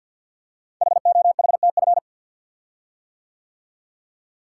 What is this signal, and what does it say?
Morse code, heard as a keyed tone, about 49 wpm, 700 Hz. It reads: HOHTF
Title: Code Groups - 5 Letters - 50wpm